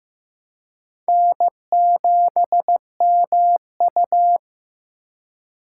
Morse code, heard as a keyed tone, about 16 wpm, 700 Hz. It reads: N7MU
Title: USA Call Signs - 4 Characters - 4x Character Spacing - 15wpm